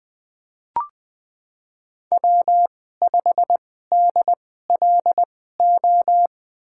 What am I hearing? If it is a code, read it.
Morse code, heard as a keyed tone, about 21 wpm, spaced as for 15 wpm, 700 Hz. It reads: W5DLO
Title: USA Call Signs - 5 Characters - 2x Character Spacing - 20wpm